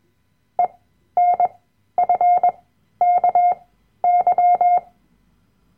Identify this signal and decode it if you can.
Morse code, heard as a keyed tone, about 21 wpm, spaced as for 12 wpm, 710 Hz. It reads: ENFKY
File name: Workshop Dateien / CW-Online-Kurs / Lektion 010